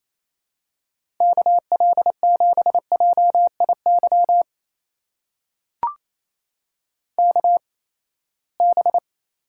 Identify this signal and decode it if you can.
Morse code, heard as a keyed tone, about 28 wpm, spaced as for 19 wpm, 700 Hz. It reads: KL7JIY KB
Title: USA Call Signs - 6 Characters - 8x Character Spacing - 28wpm